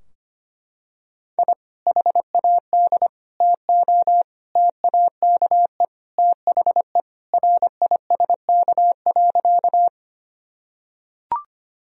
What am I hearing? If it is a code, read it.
Morse code, heard as a keyed tone, about 25 wpm, 700 Hz. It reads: I HAD TO TAKE THE RISK.
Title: Sentences from Top 1000 Words - Review Part 4 - 25wpm